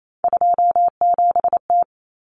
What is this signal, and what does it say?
Morse code, heard as a keyed tone, about 28 wpm, 700 Hz. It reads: 27T
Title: Common Cut Numbers - 3-Digit 28wpm